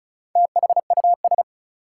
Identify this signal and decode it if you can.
Morse code, heard as a keyed tone, about 34 wpm, 700 Hz. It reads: THUS
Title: Sentences from Top 800 Words - New Word Drill - 35wpm